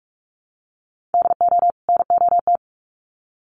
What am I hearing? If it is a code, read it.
Morse code, heard as a keyed tone, about 44 wpm, 700 Hz. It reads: DO NOT
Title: Sets of 2 Words 45wpm